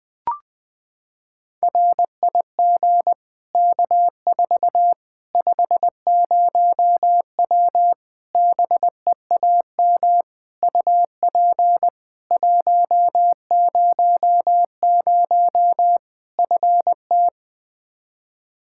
Morse code, heard as a keyed tone, about 20 wpm, 700 Hz. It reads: RIG K4 50W BEAM UP 100 FT
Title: Practice QSO - Exchange 2 - Info - 20wpm